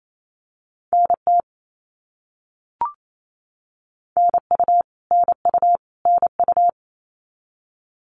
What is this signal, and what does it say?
Morse code, heard as a keyed tone, about 28 wpm, 700 Hz. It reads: NT NU NU NU
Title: Canadian Provinces and Territories - Repeated 3x - 28wpm